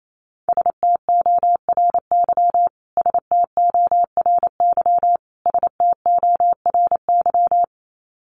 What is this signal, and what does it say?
Morse code, heard as a keyed tone, about 28 wpm, 700 Hz. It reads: STORY STORY STORY
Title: Top 300 Words Repeated 28wpm